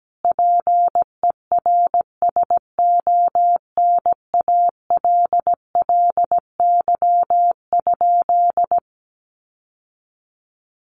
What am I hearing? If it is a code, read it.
Morse code, heard as a keyed tone, about 17 wpm, 700 Hz. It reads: PERSONALLY?
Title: Sentences from Top 1400 Words 17wpm